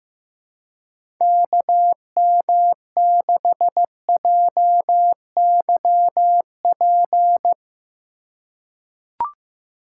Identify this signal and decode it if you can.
Morse code, heard as a keyed tone, about 15 wpm, 700 Hz. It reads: KM6JYP
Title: USA Call Signs - 6 Characters - 2x Character Spacing - 15wpm